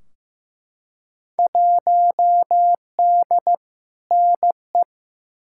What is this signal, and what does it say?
Morse code, heard as a keyed tone, about 15 wpm, 700 Hz. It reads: 1D NE
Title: Field Day Exchanges - 15wpm